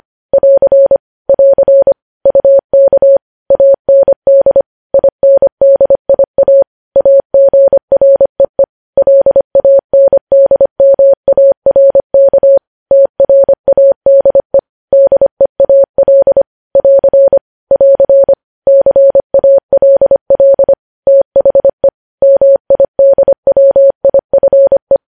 Morse code, heard as a keyed tone, about 25 wpm, 550 Hz. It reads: <AR> <AR> UK AND INDIA AGREE LANDMARK TRADE DEAL <AR> <AR> CALL THE MIDWIFE
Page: news headlines in morse code at 25 wpm